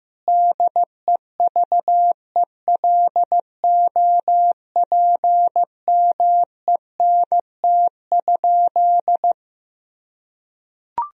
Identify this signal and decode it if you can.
Morse code, heard as a keyed tone, about 15 wpm, 700 Hz. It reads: DEVELOPMENT?
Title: Sentences from Top 1000 Words - Review Part 4 - 15wpm